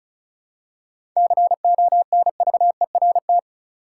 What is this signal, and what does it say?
Morse code, heard as a keyed tone, about 34 wpm, 700 Hz. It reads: CONVERT
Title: Seven Letter Words - 35wpm